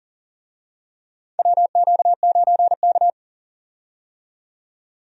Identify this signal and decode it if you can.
Morse code, heard as a keyed tone, about 38 wpm, 700 Hz. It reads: WQ9K
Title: USA Call Signs - 4 Characters - 40wpm